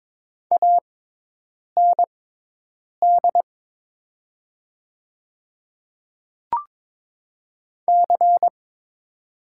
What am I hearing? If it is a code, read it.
Morse code, heard as a keyed tone, about 22 wpm, spaced as for 8 wpm, 700 Hz. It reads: AND C
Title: 6x Character Spacing - Top 300 Words - 22wpm